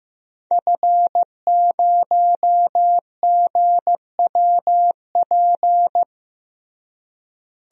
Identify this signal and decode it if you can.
Morse code, heard as a keyed tone, about 15 wpm, 700 Hz. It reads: F0GWP
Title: DXCC-100 Call Signs - 5 Characters - Repeated 2x - 15wpm